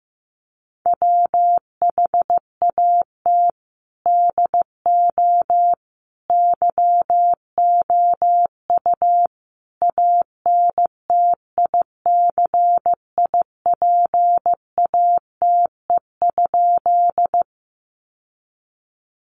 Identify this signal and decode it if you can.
Morse code, heard as a keyed tone, about 15 wpm, 700 Hz. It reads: WHAT DO YOU ANTICIPATE?
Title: Sentences from Top 1500 Words 15wpm